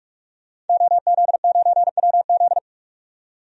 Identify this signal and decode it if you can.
Morse code, heard as a keyed tone, about 41 wpm, 700 Hz. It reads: OZ9WZ
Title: DXCC-100 Call Signs - 5 Characters - 45wpm